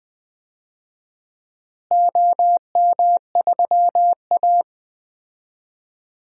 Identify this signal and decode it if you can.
Morse code, heard as a keyed tone, about 20 wpm, 700 Hz. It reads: OM3A
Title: DXCC-100 Call Signs - 4 Characters - Repeated 3x - 20wpm